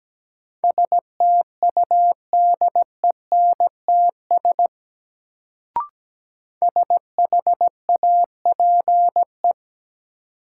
Morse code, heard as a keyed tone, about 17 wpm, 700 Hz. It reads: STUDENTS SHAPE
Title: Sets of 5 Words Encore - New Word Drill - 17wpm